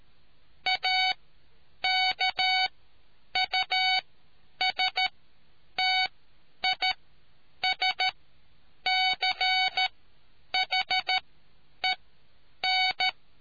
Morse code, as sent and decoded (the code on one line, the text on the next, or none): .- -.- ..- ... - .. ... -.-. .... -.
AKUSTISCHEN